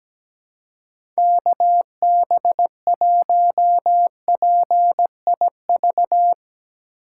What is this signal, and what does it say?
Morse code, heard as a keyed tone, about 17 wpm, 700 Hz. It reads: KB1PIV
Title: USA Call Signs - 6 Characters - 4x Character Spacing - 17wpm